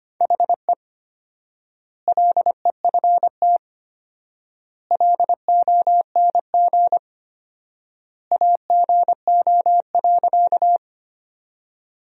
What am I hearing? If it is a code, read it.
Morse code, heard as a keyed tone, about 25 wpm, 700 Hz. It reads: HE LEFT LONG AGO.
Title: 4x Word-Spacing - Sentences from Top 500 Words - 25wpm